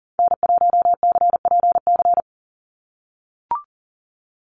Morse code, heard as a keyed tone, about 40 wpm, 700 Hz. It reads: N1CPC
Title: USA Call Signs 40wpm